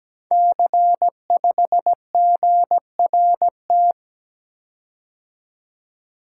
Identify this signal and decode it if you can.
Morse code, heard as a keyed tone, about 17 wpm, 700 Hz. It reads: C5GRT
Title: Code Groups - 5 Letter:Numbers - 17wpm